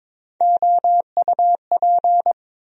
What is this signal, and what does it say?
Morse code, heard as a keyed tone, about 22 wpm, 700 Hz. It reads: OUP
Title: USA Call Sign Postfixes - 3 Letters 22wpm